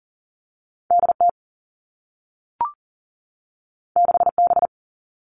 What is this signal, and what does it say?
Morse code, heard as a keyed tone, about 38 wpm, 700 Hz. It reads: DT 6B
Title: All Cut Numbers - 2-Digit 40wpm